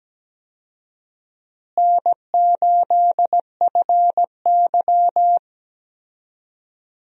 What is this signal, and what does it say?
Morse code, heard as a keyed tone, about 17 wpm, 700 Hz. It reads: N8FY
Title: USA Call Signs - 4 Characters - Repeated 3x - 17wpm